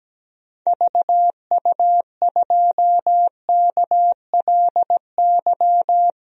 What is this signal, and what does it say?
Morse code, heard as a keyed tone, about 17 wpm, 700 Hz. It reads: VU2KLY
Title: DXCC-100 Call Signs - 6 Characters - 17wpm